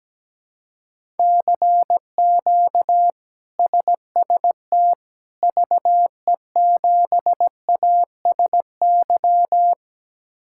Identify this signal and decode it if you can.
Morse code, heard as a keyed tone, about 17 wpm, 700 Hz. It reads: CQ SST VE7ASY
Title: K1USN SST QSO Practice - 17wpm